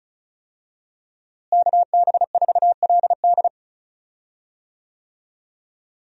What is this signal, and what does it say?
Morse code, heard as a keyed tone, about 34 wpm, 700 Hz. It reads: KB4LD
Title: USA Call Signs - 5 Characters - Repeated 2x - 35wpm